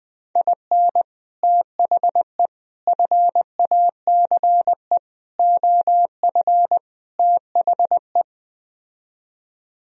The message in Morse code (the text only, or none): IN THE FACE OF THE